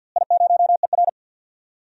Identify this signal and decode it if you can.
Morse code, heard as a keyed tone, about 49 wpm, 700 Hz. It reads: I0ER
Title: DXCC-100 Call Signs - 4 Characters - Repeated 2x - 50wpm